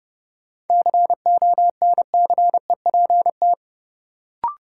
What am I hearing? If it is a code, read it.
Morse code, heard as a keyed tone, about 30 wpm, 700 Hz. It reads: CONCEPT
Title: Sentences from Top 1100 Words - New Word Drill - 30wpm